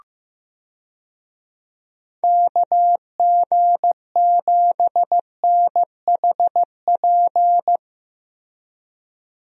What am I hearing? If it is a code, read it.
Morse code, heard as a keyed tone, about 16 wpm, 700 Hz. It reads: KG7NHP
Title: USA Call Signs - 6 Characters - Repeated 3x - 15wpm